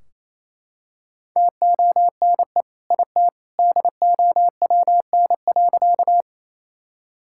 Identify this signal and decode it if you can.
Morse code, heard as a keyed tone, about 28 wpm, 700 Hz. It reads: TONE IT DOWN.
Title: Sentences from Top 1600 Words 28wpm